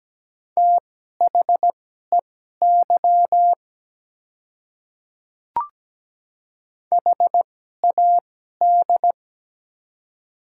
Morse code, heard as a keyed tone, about 17 wpm, spaced as for 12 wpm, 700 Hz. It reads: THEY HAD
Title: 2x Character Spacing - Top 200 Words - 17wpm